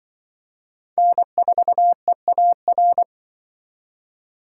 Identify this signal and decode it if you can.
Morse code, heard as a keyed tone, about 25 wpm, 700 Hz. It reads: N4EAR
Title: Speed Racing - USA 1x3 Call Signs 20wpm